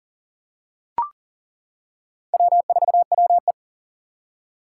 Morse code, heard as a keyed tone, about 37 wpm, 700 Hz. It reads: W4WE
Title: Four Challenging Letter-Number Combinations 40wpm